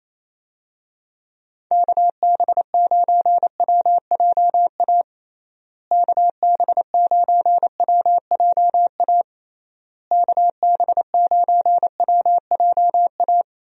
Morse code, heard as a keyed tone, about 28 wpm, 700 Hz. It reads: KB9WJA KB9WJA KB9WJA
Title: USA Call Signs - 6 Characters - Repeated 3x - 28wpm